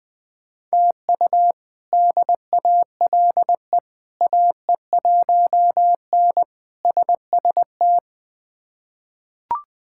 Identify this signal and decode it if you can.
Morse code, heard as a keyed tone, about 20 wpm, 700 Hz. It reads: TU DALE AE1N SST
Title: K1USN SST QSO Practice - 20wpm